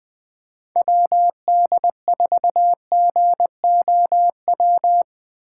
Morse code, heard as a keyed tone, about 20 wpm, 700 Hz. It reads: WD4GOW
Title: USA Call Signs - 6 Characters - 3x Character Spacing - 20wpm